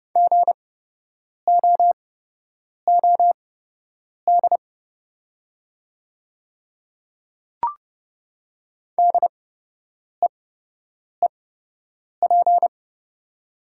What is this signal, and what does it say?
Morse code, heard as a keyed tone, about 30 wpm, spaced as for 8 wpm, 700 Hz. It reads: GOOD DEEP
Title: 8x Character Spacing - Top 500 Words - 30wpm